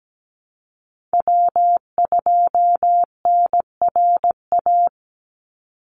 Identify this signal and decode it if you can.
Morse code, heard as a keyed tone, about 17 wpm, 700 Hz. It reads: W2NRA
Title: Call Signs Repeated 3x - 17wpm